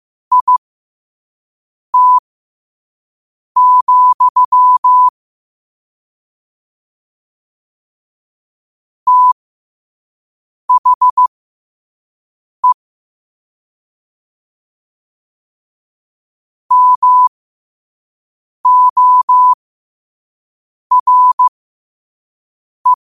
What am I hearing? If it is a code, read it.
Morse code, heard as a keyed tone, about 15 wpm, spaced as for 5 wpm, 1000 Hz. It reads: IT, THE MORE
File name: qotdfiles / 5wpm